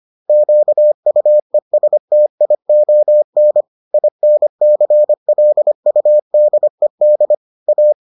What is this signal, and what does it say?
Morse code, heard as a keyed tone, about 25 wpm, 600 Hz. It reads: QUESTION INCLUDED A